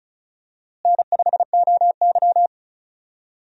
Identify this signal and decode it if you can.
Morse code, heard as a keyed tone, about 34 wpm, 700 Hz. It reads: N5OY
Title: USA Call Signs - 4 Characters - Repeated 2x - 35wpm